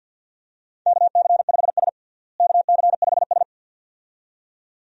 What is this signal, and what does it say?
Morse code, heard as a keyed tone, about 52 wpm, 700 Hz. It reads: KC5S KC5S
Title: USA Call Signs - 4 Characters - Repeated 2x - 50wpm